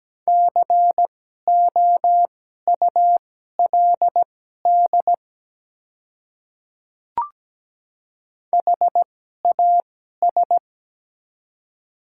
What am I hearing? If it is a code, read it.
Morse code, heard as a keyed tone, about 17 wpm, spaced as for 12 wpm, 700 Hz. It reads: COULD HAS
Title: 2x Character Spacing - Top 100 Words - 17wpm